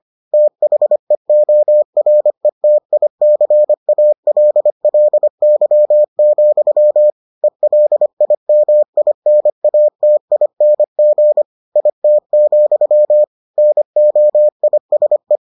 Morse code, heard as a keyed tone, about 25 wpm, 600 Hz. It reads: THEORETICALLY, ELIMINATING IT, NOISE